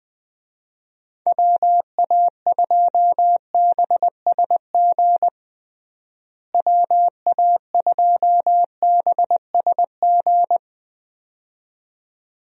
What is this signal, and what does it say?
Morse code, heard as a keyed tone, about 20 wpm, 700 Hz. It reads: WA2BSG WA2BSG
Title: USA Call Signs - 6 Characters - Repeated 2x - 20wpm